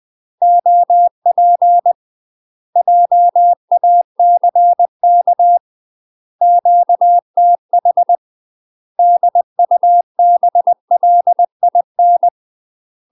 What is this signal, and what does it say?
Morse code, heard as a keyed tone, about 20 wpm, 700 Hz. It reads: OP JACK QTH DUBLIN
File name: UK MASTER QSOs TRAINING / 20WPM / UK MASTER QSOs 0036